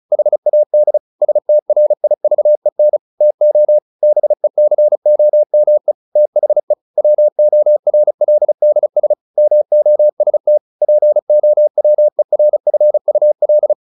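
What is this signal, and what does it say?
Morse code, heard as a keyed tone, about 34 wpm, 600 Hz. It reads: HAD STRIVEN TO BECOME THE WORLDS MOST POWERFUL
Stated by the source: Morse practice at 35 wpm from CW med Gnister, sending Ken Follett's novel Never in English.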